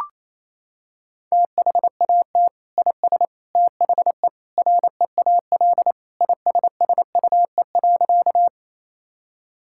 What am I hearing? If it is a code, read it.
Morse code, heard as a keyed tone, about 27 wpm, 700 Hz. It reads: THAT IS THE REAL ISSUE.
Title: Sentences from Top 1000 Words - Review Part 6 - 28wpm